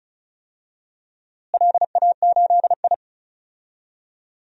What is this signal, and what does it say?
Morse code, heard as a keyed tone, about 36 wpm, 700 Hz. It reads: LA8I
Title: DXCC-100 Call Signs - 4 Characters - Repeated 3x - 35wpm